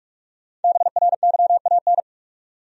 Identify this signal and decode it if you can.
Morse code, heard as a keyed tone, about 46 wpm, 700 Hz. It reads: BRYAN
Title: USA Common Names - Max 5 Letters - 45wpm